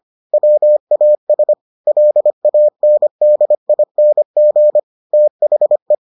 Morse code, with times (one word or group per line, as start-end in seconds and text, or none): 0.3-1.5 WAS
1.9-4.8 LANDING
5.1-5.9 THE